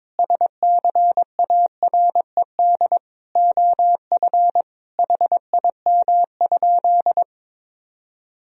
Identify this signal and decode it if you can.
Morse code, heard as a keyed tone, about 22 wpm, 700 Hz. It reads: SCARED OF HIM?
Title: Sentences from Top 1000 Words - Review Part 4 - 22wpm